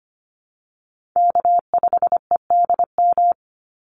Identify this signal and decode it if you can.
Morse code, heard as a keyed tone, about 25 wpm, 700 Hz. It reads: K5EDM
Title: Call Signs - 25wpm